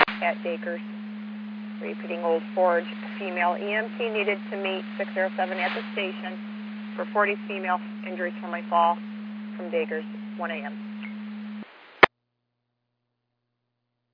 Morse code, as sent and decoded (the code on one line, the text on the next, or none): none